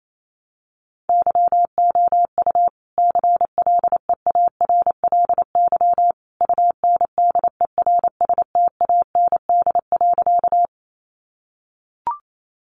Morse code, heard as a keyed tone, about 28 wpm, 700 Hz. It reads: YOU CLEARLY UNDERSTAND.
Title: Sentences from Top 1300 Words 28wpm